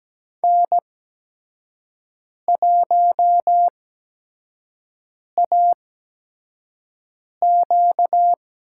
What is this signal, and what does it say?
Morse code, heard as a keyed tone, about 17 wpm, spaced as for 5 wpm, 700 Hz. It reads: N1AQ
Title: USA Call Signs - 5 Characters - 8x Character Spacing - 17wpm